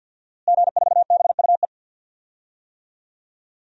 Morse code, heard as a keyed tone, about 52 wpm, 700 Hz. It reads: G4BUE
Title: DXCC-100 Call Signs - 5 Characters - Repeated 3x - 50wpm